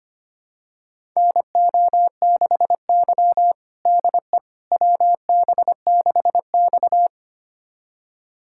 Morse code, heard as a keyed tone, about 25 wpm, 700 Hz. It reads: NO6Y DE WB6X
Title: USA Callsign DE Callsign - 4 Characters - 25wpm